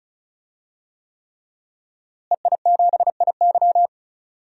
Morse code, heard as a keyed tone, about 36 wpm, 700 Hz. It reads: EI7IY
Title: DXCC-100 Call Signs - 5 Characters - Repeated 3x - 35wpm